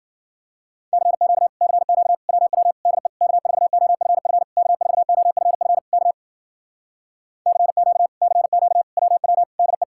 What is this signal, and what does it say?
Morse code, heard as a keyed tone, about 53 wpm, 700 Hz. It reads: CQ CQ WW DE K4OFF K4OFF K CQ CQ WW DE